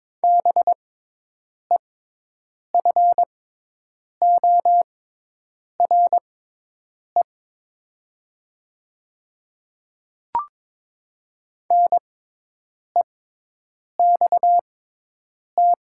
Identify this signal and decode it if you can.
Morse code, heard as a keyed tone, about 22 wpm, spaced as for 8 wpm, 700 Hz. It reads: BEFORE NEXT
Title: 6x Character Spacing - Top 300 Words - 22wpm